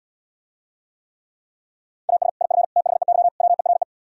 Morse code, heard as a keyed tone, about 77 wpm, 700 Hz. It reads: AS IF THEY WERE